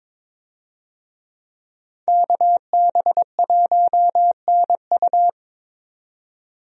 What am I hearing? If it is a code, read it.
Morse code, heard as a keyed tone, about 22 wpm, 700 Hz. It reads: KB1NU
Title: USA Call Signs - 5 Characters - 22wpm